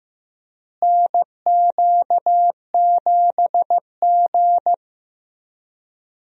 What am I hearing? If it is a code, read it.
Morse code, heard as a keyed tone, about 15 wpm, 700 Hz. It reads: NQ7G
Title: USA Call Signs - 4 Characters - Repeated 3x - 15wpm